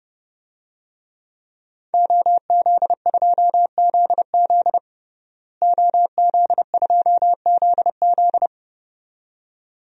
Morse code, heard as a keyed tone, about 31 wpm, 700 Hz. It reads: OZ2ZZ OZ2ZZ
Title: DXCC-100 Call Signs - 5 Characters - Repeated 2x - 30wpm